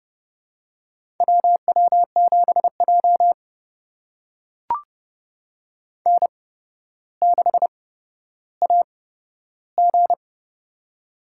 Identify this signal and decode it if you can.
Morse code, heard as a keyed tone, about 31 wpm, spaced as for 13 wpm, 700 Hz. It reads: WW7J N6AG
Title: USA Call Signs - 4 Characters - 8x Character Spacing - 30wpm